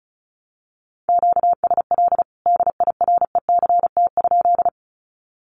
Morse code, heard as a keyed tone, about 34 wpm, 700 Hz. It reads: QSL DIRECT?
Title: Common QSO Phrases 35wpm